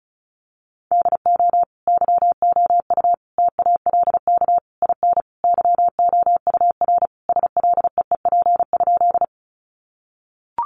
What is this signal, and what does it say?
Morse code, heard as a keyed tone, about 34 wpm, 700 Hz. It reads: DO YOU TALK IN YOUR SLEEP?
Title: Sentences from Top 500 Words 35wpm